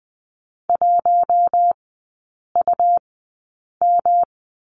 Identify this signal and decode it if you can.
Morse code, heard as a keyed tone, about 20 wpm, spaced as for 8 wpm, 690 Hz. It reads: 1UM
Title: 191 Lesson 31e - Letter v - Three Characters 20wpm